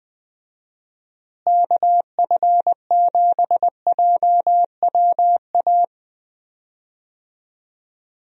Morse code, heard as a keyed tone, about 20 wpm, 700 Hz. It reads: KF7JWA
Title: USA Call Signs - 6 Characters - Repeated 2x - 20wpm